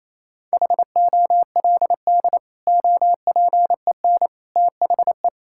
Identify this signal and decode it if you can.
Morse code, heard as a keyed tone, about 28 wpm, 700 Hz. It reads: HOLD OPEN THE
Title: Sentences from Top 1600 Words 28wpm